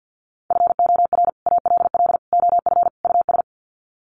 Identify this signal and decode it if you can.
Morse code, heard as a keyed tone, about 48 wpm, 700 Hz. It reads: FOR ALL OF US
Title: Sets of 4 Words 50wpm